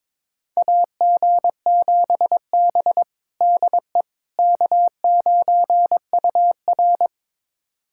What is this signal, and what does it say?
Morse code, heard as a keyed tone, about 21 wpm, 700 Hz. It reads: AG7B DE K9UR
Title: USA Callsign DE Callsign - 4 Characters - 22wpm